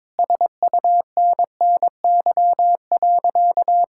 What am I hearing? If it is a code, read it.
Morse code, heard as a keyed tone, about 22 wpm, 700 Hz. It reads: SUNNY.
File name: practice / 22wpm / Sentences with 4-Word Phrases - 22wpm